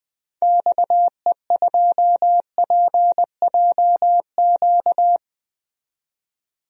Morse code, heard as a keyed tone, about 20 wpm, 700 Hz. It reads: XE2PJQ
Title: DXCC-100 Call Signs - 6 Characters - 20wpm